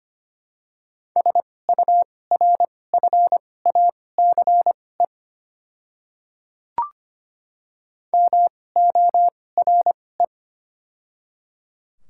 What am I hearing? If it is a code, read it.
Morse code, heard as a keyed tone, about 25 wpm, spaced as for 18 wpm, 700 Hz. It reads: SURFACE MORE
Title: 2x Character Spacing - Top 500 Words - 25wpm